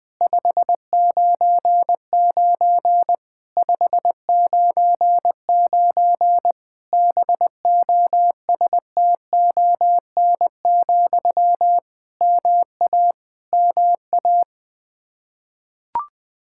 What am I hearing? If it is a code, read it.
Morse code, heard as a keyed tone, about 20 wpm, 700 Hz. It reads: 599 599 BOSTON, MA MA
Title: Practice QSO - Exchange 1 - Beginning - 20wpm